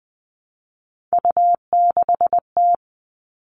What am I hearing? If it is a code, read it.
Morse code, heard as a keyed tone, about 19 wpm, 700 Hz. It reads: U6T
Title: All Cut Numbers - 3-Digit 20wpm